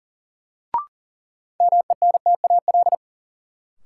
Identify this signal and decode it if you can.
Morse code, heard as a keyed tone, about 40 wpm, 700 Hz. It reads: MENTAL
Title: Sentences from Top 1000 Words - Review Part 4 - New Word Drill - 40wpm